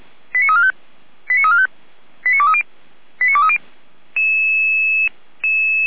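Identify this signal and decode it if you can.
Five-tone selective call (ZVEI), heard as quick runs of five tones, tones of about 70 ms. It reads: 89345 89345 89230 89230
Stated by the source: ZVEI five-tone selective call received on an SDR.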